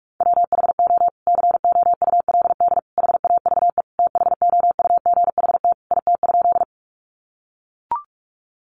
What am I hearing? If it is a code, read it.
Morse code, heard as a keyed tone, about 44 wpm, 700 Hz. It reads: WHO COULD HAVE THOUGHT IT?
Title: Sentences from Top 300 Words 45wpm